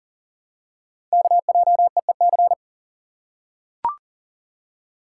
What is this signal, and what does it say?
Morse code, heard as a keyed tone, about 39 wpm, 700 Hz. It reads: KJEEC
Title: Code Groups - 5 Letters - 40wpm